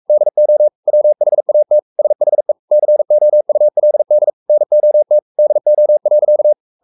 Morse code, heard as a keyed tone, about 44 wpm, 600 Hz. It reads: DO WHAT SHE COULD NOT DO.
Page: Romanen Never af Ken Follet på engelsk.